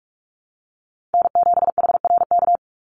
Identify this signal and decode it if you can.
Morse code, heard as a keyed tone, about 44 wpm, 700 Hz. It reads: N7HRK
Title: Call Signs - 45wpm